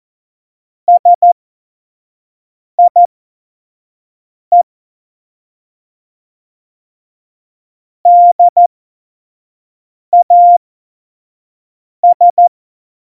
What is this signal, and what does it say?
Morse code, heard as a keyed tone, about 14 wpm, spaced as for 5 wpm, 700 Hz. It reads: SIE DAS
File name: Morsekurs / Witze / Witz 29. Oktober 2019, 25 F70